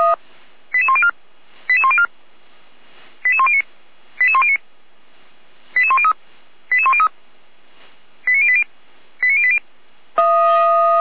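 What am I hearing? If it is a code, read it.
Five-tone selective call (ZVEI), heard as quick runs of five tones, tones of about 70 ms. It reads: 88104 88104 88109 88109 88103 88103 89080 89080
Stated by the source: Received on an SDR as a ZVEI five-tone call.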